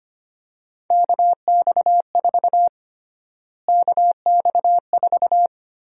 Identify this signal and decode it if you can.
Morse code, heard as a keyed tone, about 25 wpm, 700 Hz. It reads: KX4 KX4
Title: USA Call Sign Prefixes - Repeated 2x - 25wpm